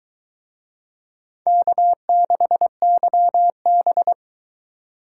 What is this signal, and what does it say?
Morse code, heard as a keyed tone, about 23 wpm, 700 Hz. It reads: K6YB